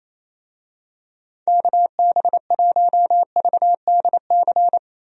Morse code, heard as a keyed tone, about 28 wpm, 700 Hz. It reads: KB1VDC
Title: USA Call Signs - 6 Characters - Repeated 3x - 28wpm